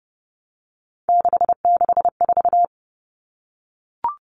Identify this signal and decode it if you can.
Morse code, heard as a keyed tone, about 30 wpm, 700 Hz. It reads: 664
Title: Common Cut Numbers - 3-Digit 30wpm